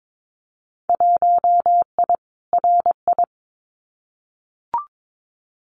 Morse code, heard as a keyed tone, about 22 wpm, 700 Hz. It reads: 1I RI
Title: Winter Field Day Exchanges Encore 22wpm